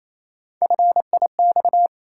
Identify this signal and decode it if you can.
Morse code, heard as a keyed tone, about 28 wpm, 700 Hz. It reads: FIX
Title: Three Letter Words - 28wpm